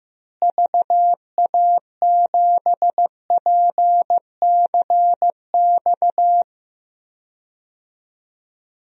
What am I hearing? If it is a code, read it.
Morse code, heard as a keyed tone, about 15 wpm, 700 Hz. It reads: VA7PCX
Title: DXCC-100 Call Signs - 6 Characters - Repeated 3x - 15wpm